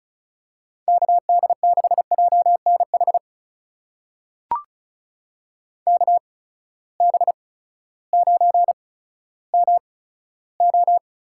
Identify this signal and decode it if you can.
Morse code, heard as a keyed tone, about 36 wpm, spaced as for 16 wpm, 700 Hz. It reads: KD6JNH KB9MO
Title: USA Call Signs - 8x Character Spacing - 35wpm